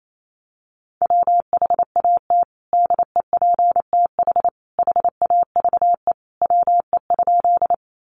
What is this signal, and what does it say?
Morse code, heard as a keyed tone, about 28 wpm, 700 Hz. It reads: WHAT DEPTH HAVE WE?
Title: Sentences from Top 1200 Words 28wpm